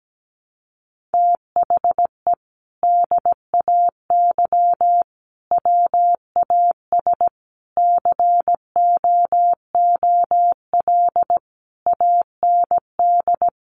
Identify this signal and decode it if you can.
Morse code, heard as a keyed tone, about 17 wpm, 700 Hz. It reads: THE DAY WAS COOL AND
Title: Sentences from Top 1200 Words 17wpm